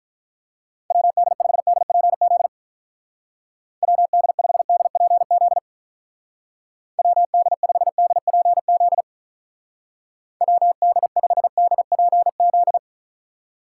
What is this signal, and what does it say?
Morse code, heard as a keyed tone, about 44 wpm, 700 Hz. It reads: WD5DPZ WD5DPZ WD5DPZ WD5DPZ